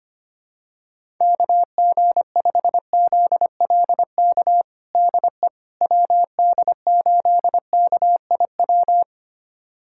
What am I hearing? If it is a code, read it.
Morse code, heard as a keyed tone, about 25 wpm, 700 Hz. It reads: KG5ZLK DE WD8KIW
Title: USA Callsign DE Callsign - 6 Characters - 25wpm